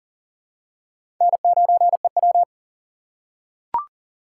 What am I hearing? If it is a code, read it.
Morse code, heard as a keyed tone, about 40 wpm, 700 Hz. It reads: N9EW
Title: USA Call Signs - 4 Characters - 3x Character Spacing - 40wpm